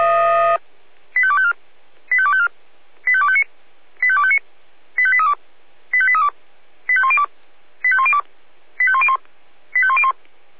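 Five-tone selective call (ZVEI), heard as quick runs of five tones, tones of about 70 ms. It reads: 85254 85254 85259 85259 85832 85832 85102 85102 85101 85101
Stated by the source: Five-tone ZVEI selective call received on SDR.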